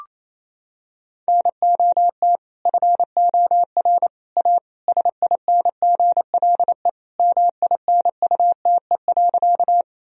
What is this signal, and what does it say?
Morse code, heard as a keyed tone, about 28 wpm, 700 Hz. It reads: NOT FOR A SINGLE MINUTE.
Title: Sentences from Top 1000 Words - Review Part 4 - 28wpm